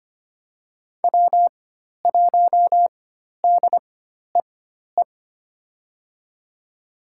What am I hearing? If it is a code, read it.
Morse code, heard as a keyed tone, about 25 wpm, spaced as for 12 wpm, 700 Hz. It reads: W1DEE
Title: USA Call Signs - 5 Characters - 4x Character Spacing - 25wpm